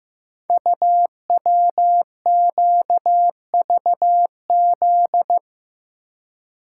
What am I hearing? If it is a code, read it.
Morse code, heard as a keyed tone, about 15 wpm, 700 Hz. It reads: UWQVZ
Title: Code Groups - 5 Letters - 15wpm